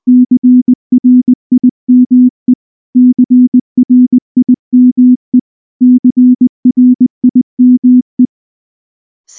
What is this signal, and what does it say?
Morse code, heard as a keyed tone, about 19 wpm, 250 Hz. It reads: CRIME CRIME CRIME
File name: morse / morse-training-20WPM-5